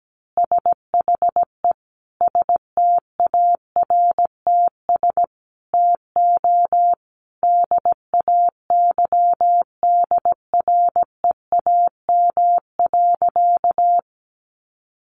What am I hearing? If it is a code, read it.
Morse code, heard as a keyed tone, about 17 wpm, 700 Hz. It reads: SHE STARTS TO DAYDREAM.
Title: Easy Phrases 17wpm